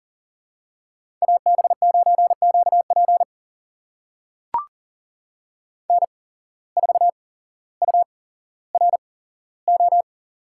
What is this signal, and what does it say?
Morse code, heard as a keyed tone, about 37 wpm, spaced as for 17 wpm, 700 Hz. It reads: AB9QP N4URO
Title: USA Call Signs - 5 Characters - 8x Character Spacing - 40wpm